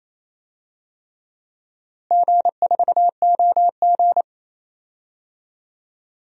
Morse code, heard as a keyed tone, about 28 wpm, 700 Hz. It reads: G4OG
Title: DXCC-100 Call Signs - 4 Characters - Repeated 3x - 28wpm